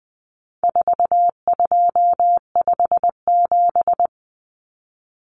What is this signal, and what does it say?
Morse code, heard as a keyed tone, about 20 wpm, 700 Hz. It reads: 4257
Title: Numbers - 4-Digit 20wpm